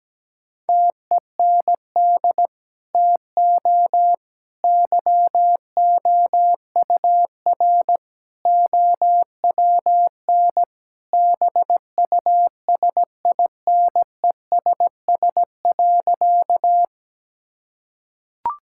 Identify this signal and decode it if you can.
Morse code, heard as a keyed tone, about 17 wpm, 700 Hz. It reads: TEND TO YOUR OWN BUSINESS.
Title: Sentences from Top 1000 Words - Review Part 5 - 17wpm